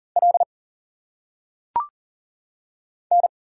Morse code, heard as a keyed tone, about 38 wpm, 700 Hz. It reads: L N
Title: Single Letters - Phonetic Pronunciation 40wpm